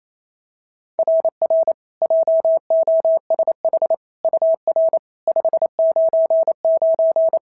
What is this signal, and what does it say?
Morse code, heard as a keyed tone, about 25 wpm, 650 Hz. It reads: RR JOSH UR 599
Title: SOTA QSO Practice 28wpm